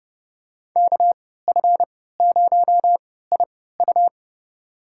Morse code, heard as a keyed tone, about 30 wpm, spaced as for 17 wpm, 700 Hz. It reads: KF0IU
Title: USA Call Signs - 5 Characters - 3x Character Spacing - 30wpm